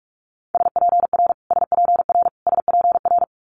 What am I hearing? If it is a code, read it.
Morse code, heard as a keyed tone, about 46 wpm, 700 Hz. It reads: SPR SPR SPR
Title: Multi-letter Phonemes Repeated 3x 45wpm